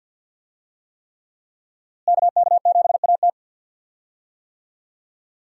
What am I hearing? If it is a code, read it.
Morse code, heard as a keyed tone, about 52 wpm, 700 Hz. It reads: KK7AT
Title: USA Call Signs - 5 Characters - Repeated 3x - 50wpm